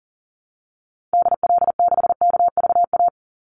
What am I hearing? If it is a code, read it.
Morse code, heard as a keyed tone, about 40 wpm, 700 Hz. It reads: DL6KVA
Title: Call Signs Repeated 2x - 40wpm